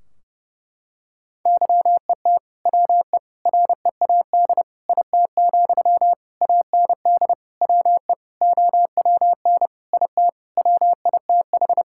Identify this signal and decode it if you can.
Morse code, heard as a keyed tone, about 32 wpm, 700 Hz. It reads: YET WE READ IT, AND WE OWN IT WITH
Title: Sentences from Top 1600 Words 30wpm